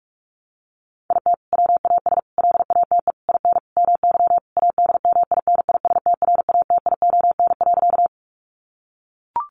Morse code, heard as a keyed tone, about 44 wpm, 700 Hz. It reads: IT WAS LATE IN MY ADMINISTRATION.
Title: Sentences from Top 1300 Words 45wpm